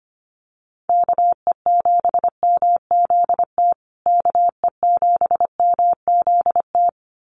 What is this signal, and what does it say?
Morse code, heard as a keyed tone, about 25 wpm, 700 Hz. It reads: KE7MZT KE7MZT
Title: USA Call Signs Repeated 2x 25wpm